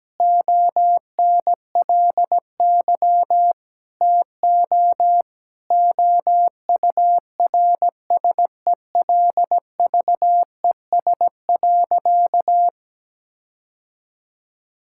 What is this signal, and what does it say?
Morse code, heard as a keyed tone, about 17 wpm, 700 Hz. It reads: ONLY TO OURSELVES.
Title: Sentences from Top 1600 Words 17wpm